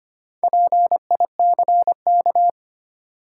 Morse code, heard as a keyed tone, about 25 wpm, 700 Hz. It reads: PICK
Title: Sentences from Top 1000 Words - New Word Drill - 25wpm